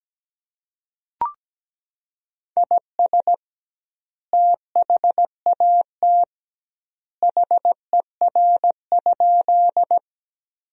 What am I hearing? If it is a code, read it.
Morse code, heard as a keyed tone, about 17 wpm, 700 Hz. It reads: IS THAT HER?
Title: 2x Word-Spacing - Sentences from Top 100 Words - 17wpm